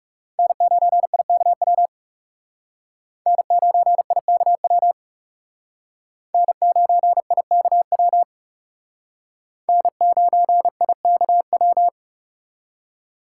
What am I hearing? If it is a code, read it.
Morse code, heard as a keyed tone, about 39 wpm, 700 Hz. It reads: N9IKW N9IKW N9IKW N9IKW